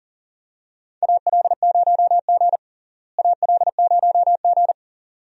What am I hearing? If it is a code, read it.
Morse code, heard as a keyed tone, about 39 wpm, 700 Hz. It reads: AL0G AL0G
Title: USA Call Signs - 4 Characters - Repeated 2x - 40wpm